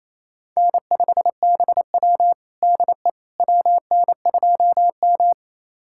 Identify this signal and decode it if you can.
Morse code, heard as a keyed tone, about 29 wpm, 700 Hz. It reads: N5BW DE WN2M
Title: USA Callsign DE Callsign - 4 Characters - 28wpm